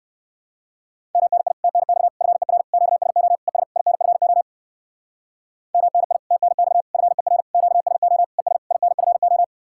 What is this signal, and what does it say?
Morse code, heard as a keyed tone, about 58 wpm, 700 Hz. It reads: MNI TNX FER QSO ES INFO MNI TNX FER QSO ES INFO